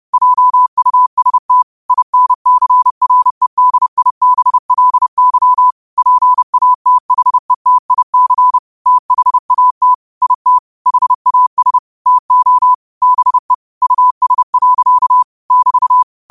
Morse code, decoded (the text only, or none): JUST INCREDIBLY PATHETIC THAT IT HAS TO BE US. <BT>